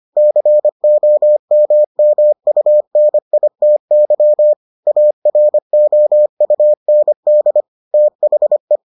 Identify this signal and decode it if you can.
Morse code, heard as a keyed tone, about 25 wpm, 600 Hz. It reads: COMMUNITY AROUND THE